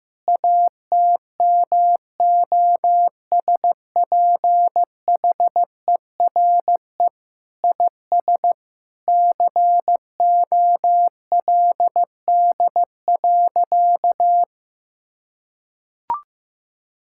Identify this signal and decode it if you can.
Morse code, heard as a keyed tone, about 15 wpm, 700 Hz. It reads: ATMOSPHERE IS COLD.
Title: Sentences from Top 1600 Words 15wpm